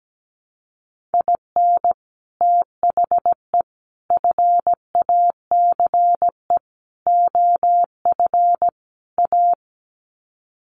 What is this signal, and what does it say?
Morse code, heard as a keyed tone, about 17 wpm, 700 Hz. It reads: IN THE FACE OF A
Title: Sets of 5 Words Encore 17wpm